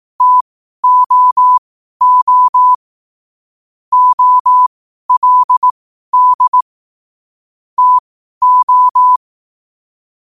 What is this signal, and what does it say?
Morse code, heard as a keyed tone, about 18 wpm, spaced as for 12 wpm, 1000 Hz. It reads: TOO OLD TO
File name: qotdfiles / 12wpm